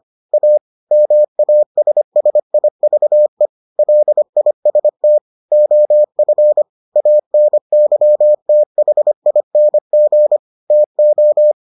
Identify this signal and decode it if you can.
Morse code, heard as a keyed tone, about 25 wpm, 600 Hz. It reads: A MASSIVE LIST OF ANYTHING TO